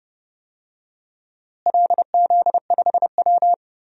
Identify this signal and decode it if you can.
Morse code, heard as a keyed tone, about 30 wpm, 700 Hz. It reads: LZ5W
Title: DXCC-100 Call Signs - 4 Characters - Repeated 2x - 30wpm